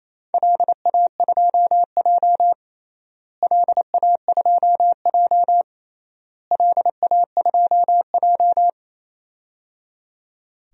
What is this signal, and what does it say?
Morse code, heard as a keyed tone, about 28 wpm, 700 Hz. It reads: LA2J LA2J LA2J
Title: DXCC-100 Call Signs - 4 Characters - Repeated 3x - 28wpm